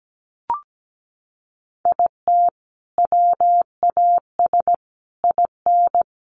Morse code, heard as a keyed tone, about 17 wpm, 700 Hz. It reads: IT WAS IN
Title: Sentences from Top 1400 Words 17wpm